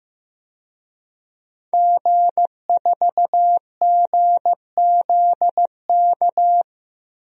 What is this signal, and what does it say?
Morse code, heard as a keyed tone, about 15 wpm, 700 Hz. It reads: G4GZK
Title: DXCC-100 Call Signs - 5 Characters - Repeated 2x - 15wpm